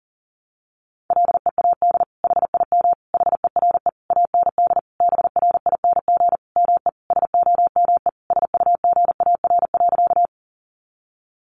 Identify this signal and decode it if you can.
Morse code, heard as a keyed tone, about 40 wpm, 700 Hz. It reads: LEAD HIM HERE AND BRING ME SOME SUGAR.
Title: Sentences from Top 1200 Words 40wpm